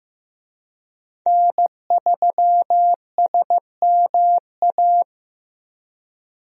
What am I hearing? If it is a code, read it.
Morse code, heard as a keyed tone, about 16 wpm, 700 Hz. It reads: N3SMA
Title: USA Call Signs - 5 Characters - Repeated 2x - 15wpm